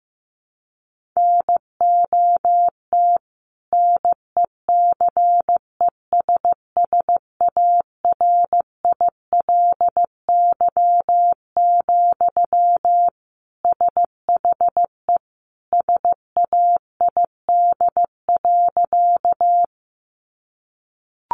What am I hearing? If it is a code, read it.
Morse code, heard as a keyed tone, about 15 wpm, 700 Hz. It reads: NOT NECESSARILY, SHE SAID.
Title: Sentences from Top 1400 Words 15wpm